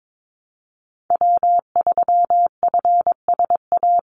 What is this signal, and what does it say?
Morse code, heard as a keyed tone, about 22 wpm, 700 Hz. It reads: W3FSA
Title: Call Signs Repeated 3x - 22wpm